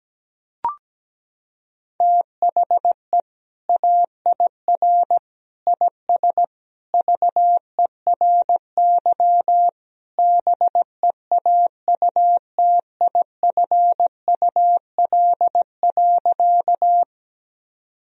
Morse code, heard as a keyed tone, about 17 wpm, 700 Hz. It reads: THE AIR IS VERY BEAUTIFUL.
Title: Sentences from Top 1000 Words - Review Part 1 - 17wpm